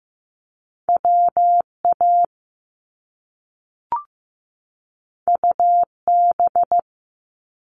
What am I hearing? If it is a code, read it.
Morse code, heard as a keyed tone, about 15 wpm, 700 Hz. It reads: WA UB
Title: Top Two Letter Combinations in Words 15wpm